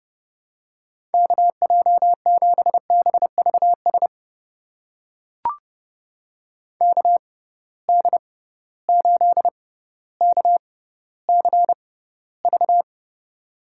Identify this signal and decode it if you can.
Morse code, heard as a keyed tone, about 30 wpm, spaced as for 15 wpm, 700 Hz. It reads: KJ7BVS KD8KCV
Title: USA Call Signs - 6 Characters - 6x Character Spacing - 30wpm